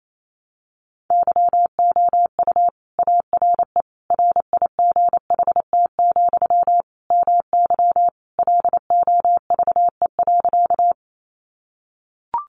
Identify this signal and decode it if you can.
Morse code, heard as a keyed tone, about 28 wpm, 700 Hz. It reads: YOU ARE RIGHT, MY LOVE.
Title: Sentences from Top 400 Words 28wpm